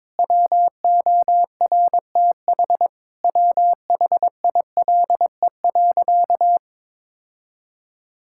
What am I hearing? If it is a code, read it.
Morse code, heard as a keyed tone, about 22 wpm, 700 Hz. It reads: WORTH WHILE.
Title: Sentences from Top 1000 Words - Review Part 5 - 22wpm